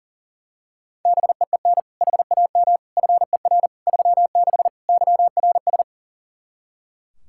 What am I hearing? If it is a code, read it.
Morse code, heard as a keyed tone, about 40 wpm, 700 Hz. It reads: BEEN HAM FER 36 YRS
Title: Practice QSO - Exchange 3 - More Info - 40wpm